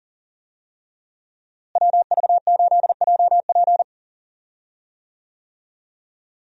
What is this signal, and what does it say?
Morse code, heard as a keyed tone, about 37 wpm, 700 Hz. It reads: WV8JP
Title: USA Call Signs - 5 Characters - Repeated 3x - 40wpm